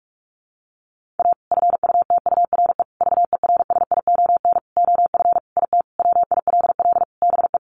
Morse code, heard as a keyed tone, about 44 wpm, 700 Hz. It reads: A FUTURE VERSION OF IT WILL BE
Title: Sentences from Top 1400 Words 45wpm